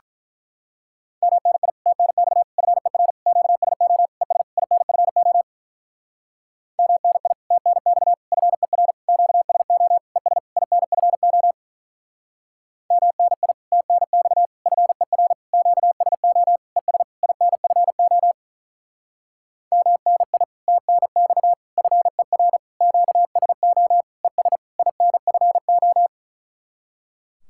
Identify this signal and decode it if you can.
Morse code, heard as a keyed tone, about 45 wpm, 700 Hz. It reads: MNI TNX FER QSO ES INFO MNI TNX FER QSO ES INFO MNI TNX FER QSO ES INFO MNI TNX FER QSO ES INFO